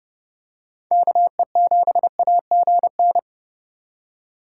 Morse code, heard as a keyed tone, about 29 wpm, 700 Hz. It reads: KE7AGN